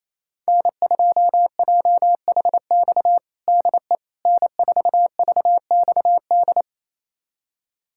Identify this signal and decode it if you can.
Morse code, heard as a keyed tone, about 27 wpm, 700 Hz. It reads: N2JHX DE N4VXD
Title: USA Callsign DE Callsign - 5 Characters - 28wpm